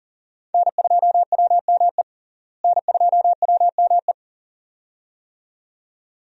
Morse code, heard as a keyed tone, about 40 wpm, 700 Hz. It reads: N2WME N2WME
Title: USA Call Signs - 5 Characters - Repeated 2x - 40wpm